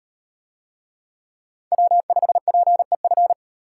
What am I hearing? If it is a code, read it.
Morse code, heard as a keyed tone, about 38 wpm, 700 Hz. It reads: W5PEF